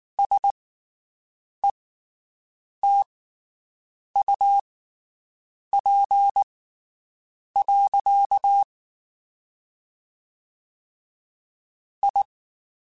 Morse code, heard as a keyed tone, about 19 wpm, spaced as for 7 wpm, 800 Hz. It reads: SETUP. I
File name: (spaced) / QSOs Spaced-mp3s / WD0BGP 1 @18 sp